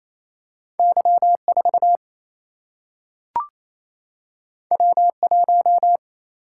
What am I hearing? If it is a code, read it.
Morse code, heard as a keyed tone, about 28 wpm, 700 Hz. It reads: Y 4 W 1
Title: Single-Letter Single-Number 28wpm